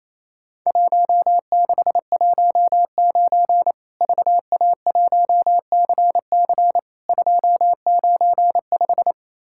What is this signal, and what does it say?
Morse code, heard as a keyed tone, about 28 wpm, 700 Hz. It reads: 1619 VA1CC 295
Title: Worked All Europe DX Contest - QTC Exchange - 28wpm